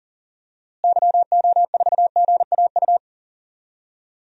Morse code, heard as a keyed tone, about 40 wpm, 700 Hz. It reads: YO4GAU
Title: DXCC-100 Call Signs - 6 Characters - Repeated 3x - 40wpm